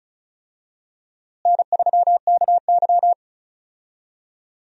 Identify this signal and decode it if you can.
Morse code, heard as a keyed tone, about 36 wpm, 700 Hz. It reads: N3KY
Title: USA Call Signs - 4 Characters - Repeated 2x - 35wpm